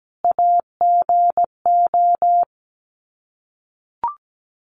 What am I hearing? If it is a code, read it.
Morse code, heard as a keyed tone, about 17 wpm, 700 Hz. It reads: AGO
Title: Sets of 5 Words 17wpm